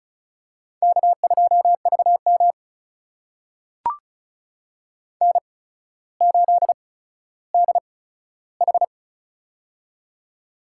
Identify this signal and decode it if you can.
Morse code, heard as a keyed tone, about 36 wpm, spaced as for 15 wpm, 700 Hz. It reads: K2VM N8DH
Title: USA Call Signs - 4 Characters - 8x Character Spacing - 35wpm